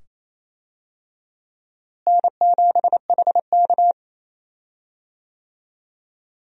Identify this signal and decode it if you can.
Morse code, heard as a keyed tone, about 28 wpm, 700 Hz. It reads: N7HK
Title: USA Call Signs - 4 Characters - Repeated 2x - 28wpm